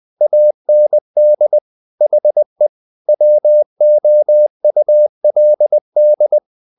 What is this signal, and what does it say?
Morse code, heard as a keyed tone, about 20 wpm, 600 Hz. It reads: AND HE WOULD
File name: the old man and the sea part 2 20wpm